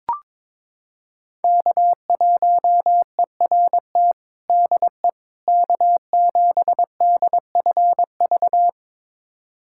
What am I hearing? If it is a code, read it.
Morse code, heard as a keyed tone, about 22 wpm, 700 Hz. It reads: K1ERT DE K7DFV
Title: USA Callsign DE Callsign - 5 Characters - 22wpm